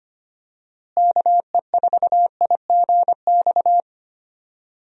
Morse code, heard as a keyed tone, about 25 wpm, 700 Hz. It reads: KE4IGX
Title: USA Call Signs - 6 Characters - Repeated 3x - 25wpm